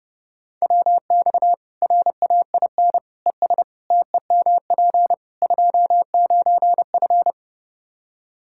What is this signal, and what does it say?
Morse code, heard as a keyed tone, about 30 wpm, 700 Hz. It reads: WX RAIN ES TEMP 29F
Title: Practice QSO - Exchange 2 - Info - 30wpm